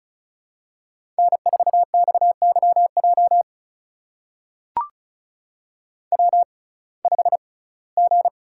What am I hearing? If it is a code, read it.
Morse code, heard as a keyed tone, about 36 wpm, spaced as for 21 wpm, 700 Hz. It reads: N4XYJ W5G
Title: USA Call Signs - 6x Character Spacing - 35wpm